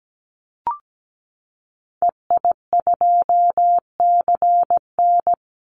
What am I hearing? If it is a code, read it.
Morse code, heard as a keyed tone, about 17 wpm, 700 Hz. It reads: EI2CN
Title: Call Signs Repeated 3x - 17wpm